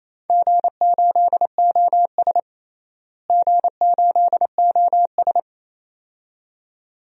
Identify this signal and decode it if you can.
Morse code, heard as a keyed tone, about 28 wpm, 700 Hz. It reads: G8OS G8OS
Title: DXCC-100 Call Signs - 4 Characters - Repeated 2x - 28wpm